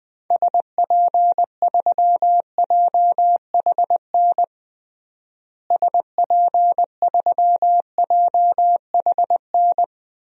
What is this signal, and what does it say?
Morse code, heard as a keyed tone, about 21 wpm, 700 Hz. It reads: SP3JHN SP3JHN
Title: DXCC-100 Call Signs - 6 Characters - Repeated 2x - 20wpm